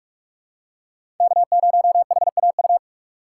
Morse code, heard as a keyed tone, about 46 wpm, 700 Hz. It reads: K0HAU
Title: USA Call Signs - 3x Character Spacing - 45wpm